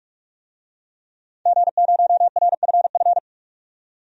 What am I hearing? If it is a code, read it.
Morse code, heard as a keyed tone, about 46 wpm, 700 Hz. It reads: G0RFF
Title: DXCC-100 Call Signs - 5 Characters - 45wpm